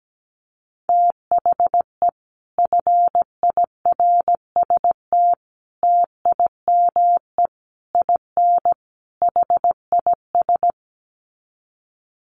Morse code, heard as a keyed tone, about 17 wpm, 700 Hz. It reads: THE FIRST TIME IN HIS
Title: Sets of 5 Words Encore2 17wpm